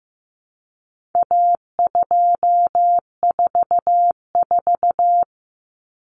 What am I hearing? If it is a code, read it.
Morse code, heard as a keyed tone, about 15 wpm, 700 Hz. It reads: A244
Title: Common Cut Numbers - 4-Digit 15wpm